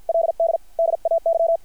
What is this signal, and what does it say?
Morse code, heard as a keyed tone, about 50 wpm, 660 Hz. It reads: LD DAY
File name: archived / CRSnet